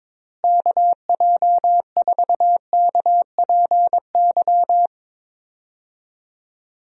Morse code, heard as a keyed tone, about 22 wpm, 700 Hz. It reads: KJ4KPY
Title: USA Call Signs - 6 Characters - Repeated 2x - 22wpm